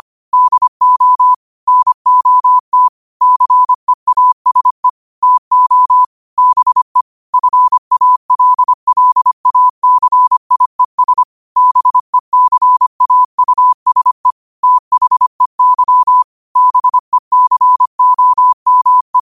Morse code, heard as a keyed tone, about 25 wpm, 1000 Hz. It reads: DO NOT CEASE TO BE FALLACIES BECAUSE THEY BECOME